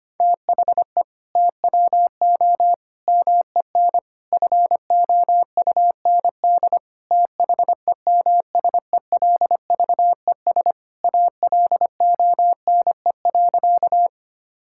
Morse code, heard as a keyed tone, about 25 wpm, 700 Hz. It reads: THE TWO MEN FOUND THEMSELVES ALONE.
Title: Sentences from Top 1000 Words - Review Part 5 - 25wpm